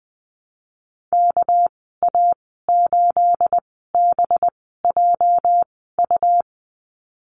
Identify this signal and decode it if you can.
Morse code, heard as a keyed tone, about 20 wpm, spaced as for 14 wpm, 700 Hz. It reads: KA8BJU
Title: 213 Lesson 34g - Number 8 - Call Signs 20wpm